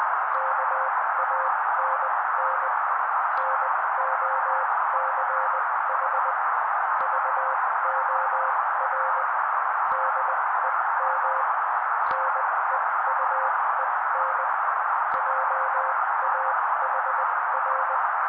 Morse code, heard as a keyed tone, about 20 wpm, spaced as for 16 wpm, 530 Hz. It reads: KANN NOCH VOR DEM NEUEN JAHR
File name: daily / 16wpm noise